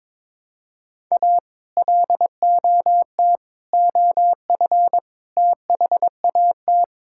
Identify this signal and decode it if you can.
Morse code, heard as a keyed tone, about 21 wpm, 700 Hz. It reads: A LOT OF THAT
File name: practice / 22wpm / Sets of 8 Words - 22wpm